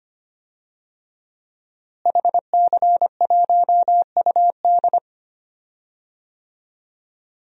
Morse code, heard as a keyed tone, about 25 wpm, 700 Hz. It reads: HC1UD
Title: DXCC-100 Call Signs - 5 Characters - Repeated 3x - 25wpm